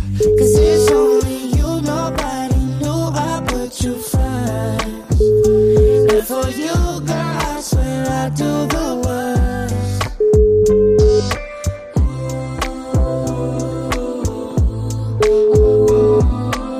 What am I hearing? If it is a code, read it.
Ringback tone (Europe, 425 Hz), with noise.